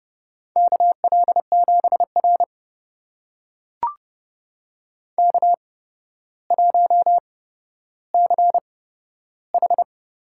Morse code, heard as a keyed tone, about 30 wpm, spaced as for 13 wpm, 700 Hz. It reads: KL7R K1CH
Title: USA Call Signs - 4 Characters - 8x Character Spacing - 30wpm